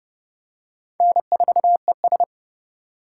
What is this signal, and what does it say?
Morse code, heard as a keyed tone, about 28 wpm, 700 Hz. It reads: N4ES